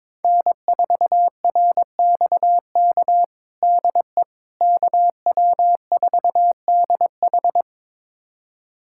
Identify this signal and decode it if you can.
Morse code, heard as a keyed tone, about 22 wpm, 700 Hz. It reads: N4RXK DE KW4DH
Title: USA Callsign DE Callsign - 5 Characters - 22wpm